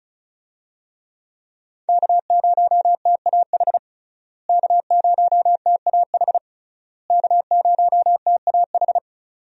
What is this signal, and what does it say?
Morse code, heard as a keyed tone, about 36 wpm, 700 Hz. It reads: K0TAH K0TAH K0TAH
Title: USA Call Signs - 5 Characters - Repeated 3x - 35wpm